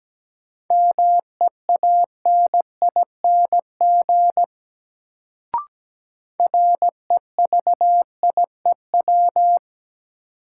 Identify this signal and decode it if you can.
Morse code, heard as a keyed tone, about 17 wpm, 700 Hz. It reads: MEANING REVIEW
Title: Sentences from Top 900 Words - New Word Drill - 17wpm